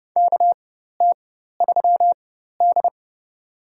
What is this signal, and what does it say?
Morse code, heard as a keyed tone, about 31 wpm, spaced as for 14 wpm, 700 Hz. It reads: KT3D
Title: USA Call Signs - 4x Character Spacing - 30wpm